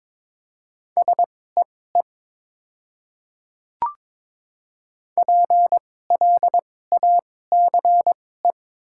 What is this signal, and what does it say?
Morse code, heard as a keyed tone, about 22 wpm, spaced as for 16 wpm, 700 Hz. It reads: SEE PLACE
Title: 2x Character Spacing - Top 300 Words - 22wpm